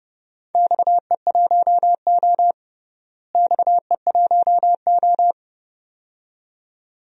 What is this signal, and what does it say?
Morse code, heard as a keyed tone, about 29 wpm, 700 Hz. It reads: XE1O XE1O
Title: DXCC-100 Call Signs - 4 Characters - Repeated 2x - 30wpm